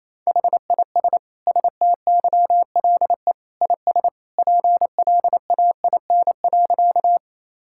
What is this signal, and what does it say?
Morse code, heard as a keyed tone, about 28 wpm, 700 Hz. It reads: HIS STYLE IS PLAIN.
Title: Sentences from Top 1000 Words - Review Part 5 - 28wpm